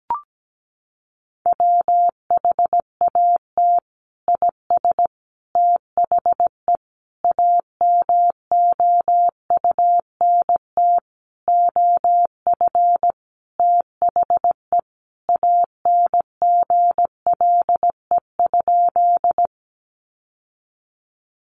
Morse code, heard as a keyed tone, about 17 wpm, 700 Hz. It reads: WHAT IS THE AMOUNT OF THE ANGLE?
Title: Sentences from Top 1500 Words 17wpm